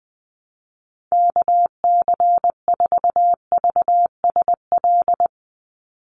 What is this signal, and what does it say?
Morse code, heard as a keyed tone, about 20 wpm, 700 Hz. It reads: KC4VSL
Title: USA Call Signs 20wpm